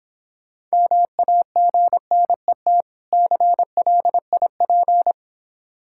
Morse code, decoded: MAGNET CLIP

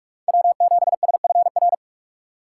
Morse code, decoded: W7SFR

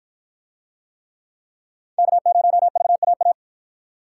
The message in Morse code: K0VAA

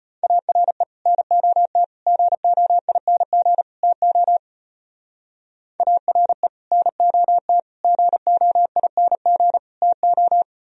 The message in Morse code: ARE NOT GOING TO ARE NOT GOING TO

